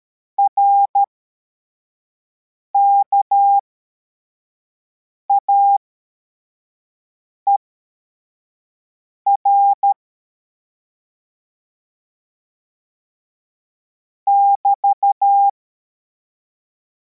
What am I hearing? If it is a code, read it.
Morse code, heard as a keyed tone, about 13 wpm, spaced as for 4 wpm, 800 Hz. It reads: RKAER <BT>